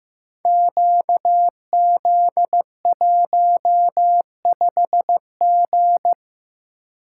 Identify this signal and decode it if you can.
Morse code, heard as a keyed tone, about 15 wpm, 700 Hz. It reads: QZ15G
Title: Code Groups - 5 Letter:Numbers - 15wpm